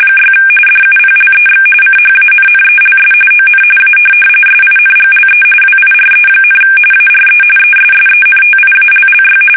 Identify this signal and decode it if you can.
RTTY, about 45.45 baud. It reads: 6 DUPLEX CHANNELS ON A SINGLE WIRE, ARE ESSENTIALLY